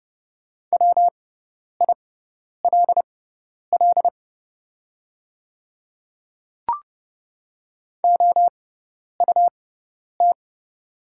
Morse code, heard as a keyed tone, about 30 wpm, spaced as for 10 wpm, 690 Hz. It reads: WILL OUT
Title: 6x Character Spacing - Top 100 Words - 30wpm